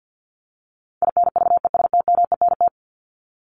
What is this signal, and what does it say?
Morse code, heard as a keyed tone, about 50 wpm, 700 Hz. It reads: INVESTMENT
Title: Top 501-750 Words 50wpm